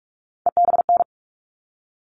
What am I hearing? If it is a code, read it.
Morse code, heard as a keyed tone, about 44 wpm, 700 Hz. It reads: EBN
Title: All Cut Numbers - 3-Digit 45wpm